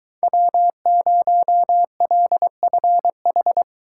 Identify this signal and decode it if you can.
Morse code, heard as a keyed tone, about 22 wpm, 700 Hz. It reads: W0LFH